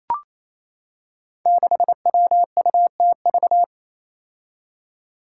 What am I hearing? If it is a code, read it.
Morse code, heard as a keyed tone, about 28 wpm, 700 Hz. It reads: 6WUTV
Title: Five Challenging Letter-Number Combinations 28wpm